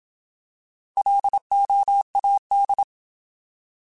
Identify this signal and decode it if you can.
Morse code, heard as a keyed tone, about 27 wpm, 800 Hz. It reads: LOAD